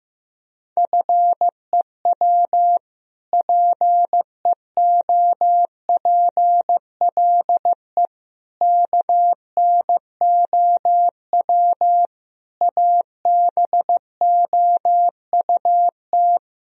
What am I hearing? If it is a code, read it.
Morse code, heard as a keyed tone, about 15 wpm, 700 Hz. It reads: FEW PEOPLE KNOW ABOUT
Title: Sentences from Top 1000 Words - Review Part 1 - 15wpm